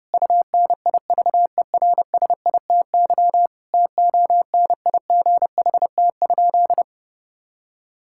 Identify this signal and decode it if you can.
Morse code, heard as a keyed tone, about 32 wpm, 700 Hz. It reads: UNIVERSITY TONIGHT?
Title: Sentences from Top 1000 Words - Review Part 5 - 30wpm